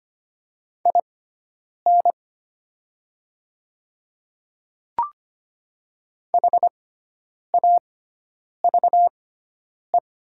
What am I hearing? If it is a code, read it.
Morse code, heard as a keyed tone, about 25 wpm, spaced as for 9 wpm, 700 Hz. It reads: IN HAVE
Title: 6x Character Spacing - Top 100 Words - 25wpm